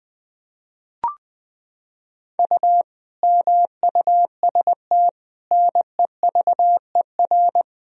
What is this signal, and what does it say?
Morse code, heard as a keyed tone, about 20 wpm, 700 Hz. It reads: U MUST NEVER
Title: Sentences from Top 300 Words and Abbreviations 20wpm